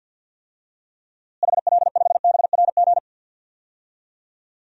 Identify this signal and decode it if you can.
Morse code, heard as a keyed tone, about 52 wpm, 700 Hz. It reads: HL5BRG
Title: DXCC-100 Call Signs - 6 Characters - Repeated 3x - 50wpm